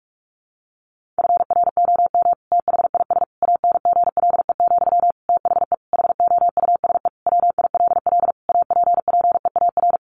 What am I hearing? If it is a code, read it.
Morse code, heard as a keyed tone, about 44 wpm, 700 Hz. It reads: FROM THIS ANGLE, THE HOUSE WILL APPEAR
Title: Sentences from Top 1500 Words 45wpm